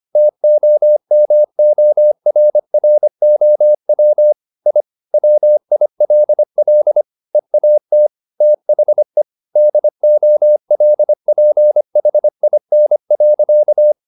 Morse code, heard as a keyed tone, about 25 wpm, 600 Hz. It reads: TOMORROW I WILL EAT THE DOLPHIN.